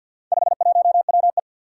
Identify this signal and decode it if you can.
Morse code, heard as a keyed tone, about 48 wpm, 700 Hz. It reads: 51WE
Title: Four Challenging Letter-Number Combinations 50wpm